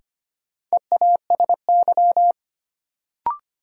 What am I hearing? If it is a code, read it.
Morse code, heard as a keyed tone, about 25 wpm, 700 Hz. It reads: EASY
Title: Sentences from Top 700 Words - New Word Drill - 25wpm